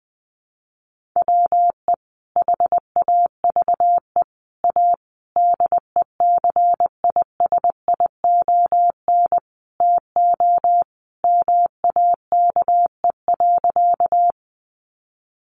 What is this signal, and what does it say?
Morse code, heard as a keyed tone, about 20 wpm, 700 Hz. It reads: WE HAVE A DECISION TO MAKE.
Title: Sentences from Top 1000 Words 20wpm